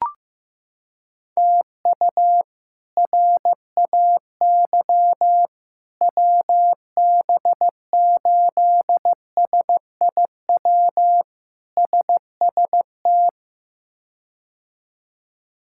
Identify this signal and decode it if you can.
Morse code, heard as a keyed tone, about 15 wpm, 700 Hz. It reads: TU RAY WB8SIW SST
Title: K1USN SST QSO Practice - 15wpm